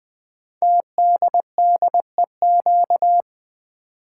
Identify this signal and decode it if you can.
Morse code, heard as a keyed tone, about 20 wpm, 700 Hz. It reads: TDDEQ
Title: Code Groups - 5 Letters - 20wpm